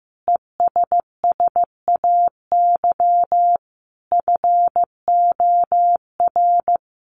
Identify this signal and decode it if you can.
Morse code, heard as a keyed tone, about 15 wpm, 700 Hz. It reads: ESSAY FOR
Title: Sentences from Top 1500 Words 15wpm